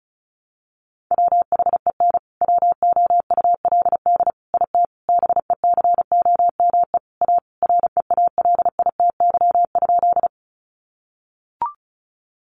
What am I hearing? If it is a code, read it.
Morse code, heard as a keyed tone, about 34 wpm, 700 Hz. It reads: WHEN WOULD IT BECOME A REALITY?
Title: Sentences from Top 1200 Words 35wpm